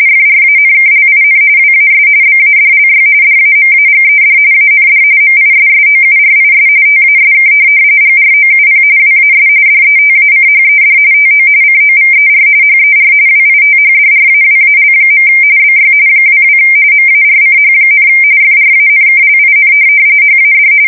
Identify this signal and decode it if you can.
RTTY, about 45.45 baud. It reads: FRANKLIN WAS BADLY DAMAGED AND MISSOURIS TASK GROUP WAS DETACHED TO COVER HER WITHDRAWAL. BY 22 MARCH, FRANKLIN HAD
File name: fsk170 / USS-Missouri